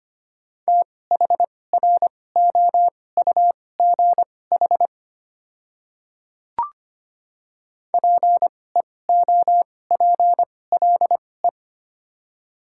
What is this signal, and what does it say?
Morse code, heard as a keyed tone, about 25 wpm, spaced as for 18 wpm, 700 Hz. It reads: THROUGH PEOPLE
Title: 2x Character Spacing - Top 200 Words - 25wpm